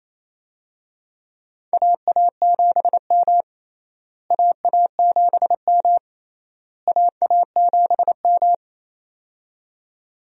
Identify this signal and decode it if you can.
Morse code, heard as a keyed tone, about 28 wpm, 700 Hz. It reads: AA7M AA7M AA7M
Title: USA Call Signs - 4 Characters - Repeated 3x - 28wpm